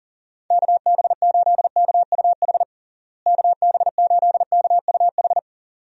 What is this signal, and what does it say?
Morse code, heard as a keyed tone, about 40 wpm, 700 Hz. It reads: KB8KUH KB8KUH
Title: USA Call Signs - 6 Characters - Repeated 2x - 40wpm